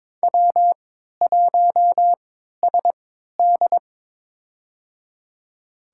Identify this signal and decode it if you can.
Morse code, heard as a keyed tone, about 22 wpm, spaced as for 13 wpm, 700 Hz. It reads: W1SD
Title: USA Call Signs - 4 Characters - 3x Character Spacing - 22wpm